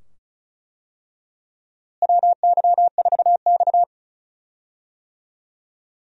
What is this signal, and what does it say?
Morse code, heard as a keyed tone, about 36 wpm, 700 Hz. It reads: WY4X
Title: USA Call Signs - 4 Characters - Repeated 2x - 35wpm